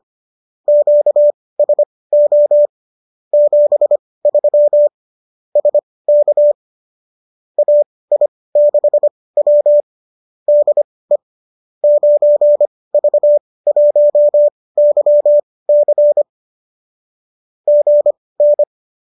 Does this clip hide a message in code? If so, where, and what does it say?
Yes: Morse code from 0.7 to 18.6 s, heard as a keyed tone, about 25 wpm, spaced as for 18 wpm, 590 Hz. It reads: QSO 73 SK AI6W DE 9V1YC GN